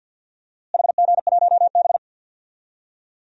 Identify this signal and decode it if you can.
Morse code, heard as a keyed tone, about 50 wpm, 700 Hz. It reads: HG1B
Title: DXCC-100 Call Signs - 4 Characters - Repeated 3x - 50wpm